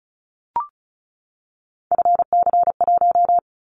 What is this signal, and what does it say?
Morse code, heard as a keyed tone, about 36 wpm, 700 Hz. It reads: FC1
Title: Callsign Prefixes for Top 50 DXCC - 35wpm